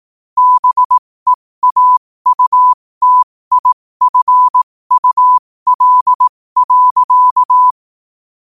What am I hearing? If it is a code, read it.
Morse code, heard as a keyed tone, about 18 wpm, spaced as for 15 wpm, 1000 Hz. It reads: BEAUTIFUL.
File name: qotdfiles / 15wpm